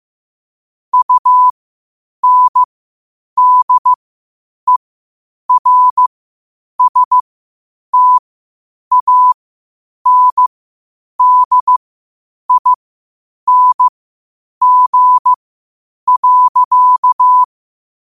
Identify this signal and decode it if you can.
Morse code, heard as a keyed tone, about 15 wpm, spaced as for 8 wpm, 1000 Hz. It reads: UNDERSTANDING.
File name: qotdfiles / 8wpm